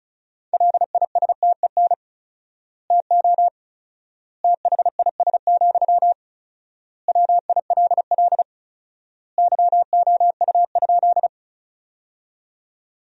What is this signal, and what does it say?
Morse code, heard as a keyed tone, about 34 wpm, 700 Hz. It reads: LISTEN TO THIS, WILL YOU?
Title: 4x Word-Spacing - Sentences from Top 400 Words - 35wpm